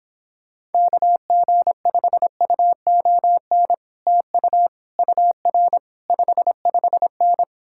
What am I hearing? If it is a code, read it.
Morse code, heard as a keyed tone, about 29 wpm, 700 Hz. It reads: KG5UON TU UR 55N